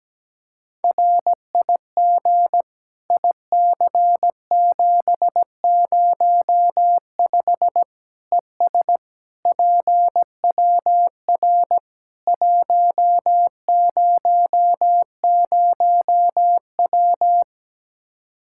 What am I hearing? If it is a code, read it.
Morse code, heard as a keyed tone, about 17 wpm, 700 Hz. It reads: RIG IC705 ES PWR 100W
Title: Practice QSO - Exchange 2 - Info - 17wpm